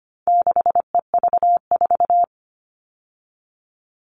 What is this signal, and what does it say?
Morse code, heard as a keyed tone, about 25 wpm, 700 Hz. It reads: 6EV4
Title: All Cut Numbers - 4-Digit 25wpm